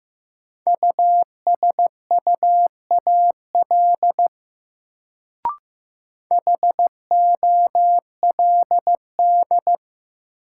Morse code, heard as a keyed tone, about 16 wpm, 700 Hz. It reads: USUAL HOLD
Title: Sentences from Top 400 Words - New Word Drill - 15wpm